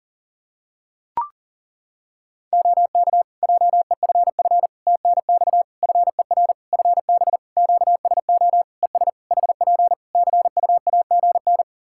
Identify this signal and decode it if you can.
Morse code, heard as a keyed tone, about 40 wpm, 700 Hz. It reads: OK JEFF TNX FER FB QSO ES HP CUAGN
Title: Practice QSO - Exchange 4 - Ending - 40wpm